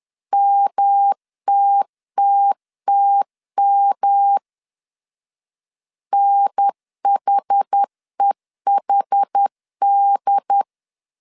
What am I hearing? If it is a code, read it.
Morse code, heard as a keyed tone, about 10 wpm, 800 Hz. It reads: MTTTM NHEHD
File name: CW / DET / 1 3 1500C S C 10wpm 3335